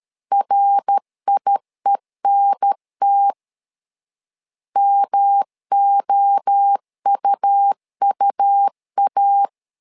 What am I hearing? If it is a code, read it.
Morse code, heard as a keyed tone, about 13 wpm, 800 Hz. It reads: RIENT MOUUA
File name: LEZIONI / 3 MWRO / 3 MWRO 12wpm 1730